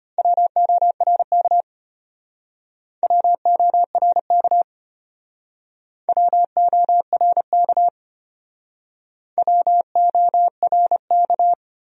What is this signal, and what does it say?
Morse code, heard as a keyed tone, about 33 wpm, 700 Hz. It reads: WORK WORK WORK WORK